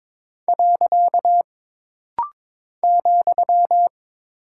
Morse code, heard as a keyed tone, about 22 wpm, 700 Hz. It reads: . ,
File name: practice / 22wpm / Single Punctuation-Prosign - Rapid-Fire - 22wpm